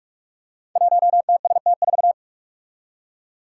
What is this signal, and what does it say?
Morse code, heard as a keyed tone, about 44 wpm, 700 Hz. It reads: 1TST4
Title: Five Challenging Letter-Number Combinations 45wpm